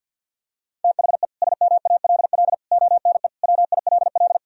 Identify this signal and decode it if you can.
Morse code, heard as a keyed tone, about 51 wpm, 700 Hz. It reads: THE SMALL ONE WILL
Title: Sentences from Top 1000 Words - Review Part 1 - 50wpm